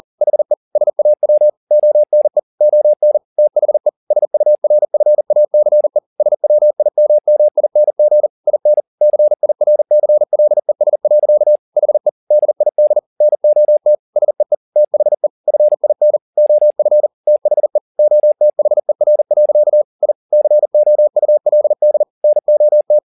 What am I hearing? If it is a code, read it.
Morse code, heard as a keyed tone, about 40 wpm, 600 Hz. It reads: HE SAW ONE ON THE SURFACE SWIMMING IN CIRCLES. HE DID NOT SEE THE FIN OF THE OTHER. I COULD NOT
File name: the old man and the sea 2 40wpm